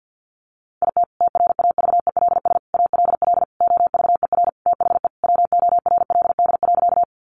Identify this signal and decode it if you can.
Morse code, heard as a keyed tone, about 50 wpm, 700 Hz. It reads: IT TRAVELS ALL OVER THE WORLD.
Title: Additional Easy Phrases 50wpm